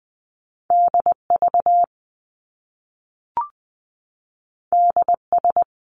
Morse code, heard as a keyed tone, about 20 wpm, 700 Hz. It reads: DV DS
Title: Top Two Letter Combinations in Words 20wpm